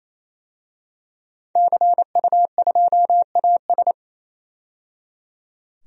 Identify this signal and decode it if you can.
Morse code, heard as a keyed tone, about 28 wpm, 700 Hz. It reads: CU2AS
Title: DXCC-100 Call Signs - 5 Characters - 28wpm